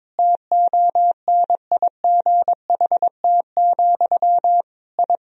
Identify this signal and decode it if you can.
Morse code, heard as a keyed tone, about 21 wpm, 700 Hz. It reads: TONIGHT, I
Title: Sentences from Top 1000 Words - Review Part 3 - 22wpm